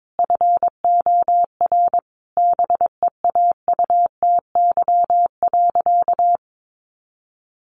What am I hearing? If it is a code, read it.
Morse code, heard as a keyed tone, about 22 wpm, 700 Hz. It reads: FOR BEAUTY.
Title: Sentences from Top 1200 Words 22wpm